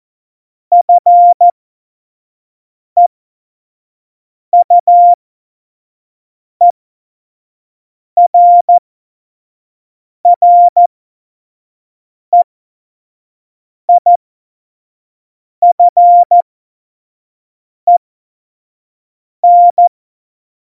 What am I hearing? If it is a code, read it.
Morse code, heard as a keyed tone, about 14 wpm, spaced as for 5 wpm, 700 Hz. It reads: FEUERREIFEN
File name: Morsekurs / Witze / Witz 14. August 2018, 25 F70